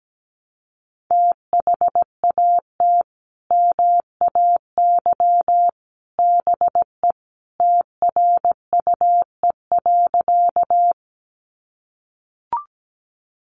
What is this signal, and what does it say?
Morse code, heard as a keyed tone, about 17 wpm, 700 Hz. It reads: THAT MAY BE TRUE.
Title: Sentences from Top 400 Words 17wpm